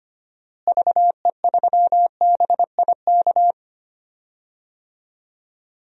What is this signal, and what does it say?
Morse code, heard as a keyed tone, about 25 wpm, 700 Hz. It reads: VE3BIK
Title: DXCC-100 Call Signs - 6 Characters - Repeated 3x - 25wpm